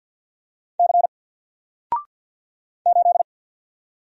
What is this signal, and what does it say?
Morse code, heard as a keyed tone, about 47 wpm, 700 Hz. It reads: C 8
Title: Single Letter-Number Alt 50wpm